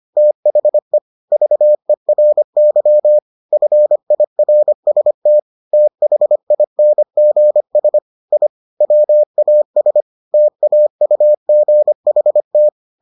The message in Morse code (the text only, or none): THE VERY FIRST THINGS I WAS TAUGHT